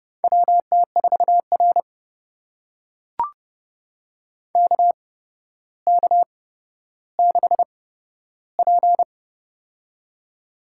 Morse code, heard as a keyed tone, about 30 wpm, spaced as for 13 wpm, 700 Hz. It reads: WT4R KK6P
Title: USA Call Signs - 4 Characters - 8x Character Spacing - 30wpm